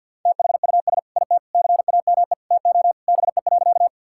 Morse code, heard as a keyed tone, about 50 wpm, 700 Hz. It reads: THUS IT CAME TO BE.
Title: Sentences from Top 1000 Words - Review Part 5 - 50wpm